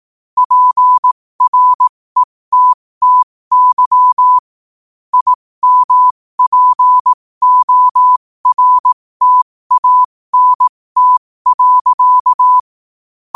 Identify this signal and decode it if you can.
Morse code, heard as a keyed tone, about 18 wpm, spaced as for 15 wpm, 1000 Hz. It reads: PRETTY IMPORTANT.
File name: qotdfiles / 15wpm